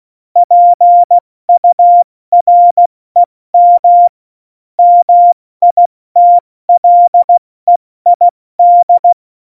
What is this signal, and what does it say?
Morse code, heard as a keyed tone, about 16 wpm, 700 Hz. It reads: PUREM MITLEID
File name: Morsekurs / Witze / Witz 04 Juli 17, Tempo70 F80